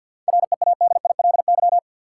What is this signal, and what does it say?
Morse code, heard as a keyed tone, about 52 wpm, 700 Hz. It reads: READILY
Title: Seven Letter Words - 50wpm